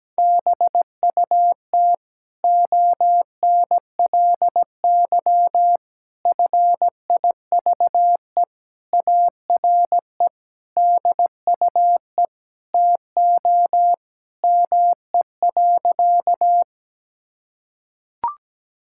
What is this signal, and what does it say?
Morse code, heard as a keyed tone, about 17 wpm, 700 Hz. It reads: BUT ONLY FIVE ARE DUE TO ME.
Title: Sentences from Top 1000 Words - Review Part 6 - 17wpm